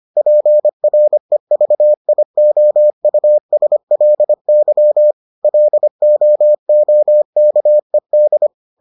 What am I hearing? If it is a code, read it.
Morse code, heard as a keyed tone, about 25 wpm, 600 Hz. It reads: PREVIOUSLY LOOKED